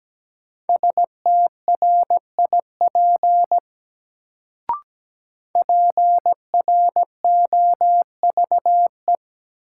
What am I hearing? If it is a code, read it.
Morse code, heard as a keyed tone, about 17 wpm, 700 Hz. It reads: STRIP PROVE
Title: Sentences from Top 1600 Words - New Word Drill - 17wpm